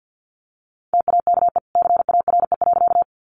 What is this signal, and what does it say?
Morse code, heard as a keyed tone, about 50 wpm, 700 Hz. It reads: TAKE CARE.
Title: Sentences from Top 300 Words 50wpm